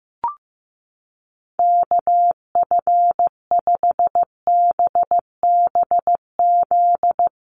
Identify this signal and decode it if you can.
Morse code, heard as a keyed tone, about 15 wpm, 700 Hz. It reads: KF5BBZ
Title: USA Call Signs 15wpm